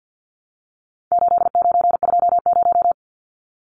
Morse code, heard as a keyed tone, about 53 wpm, 700 Hz. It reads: 8920
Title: Numbers - 4-Digit 50wpm